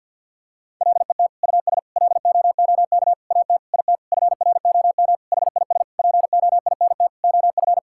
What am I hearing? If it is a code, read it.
Morse code, heard as a keyed tone, about 50 wpm, 700 Hz. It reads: LET US LOOK AT IT FROM HIS POINT OF
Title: Sentences from Top 1000 Words - Review Part 6 - 50wpm